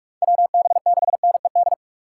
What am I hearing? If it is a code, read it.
Morse code, heard as a keyed tone, about 53 wpm, 700 Hz. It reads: WB6NED
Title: Speed Racing - USA 2x3 Call Signs 45wpm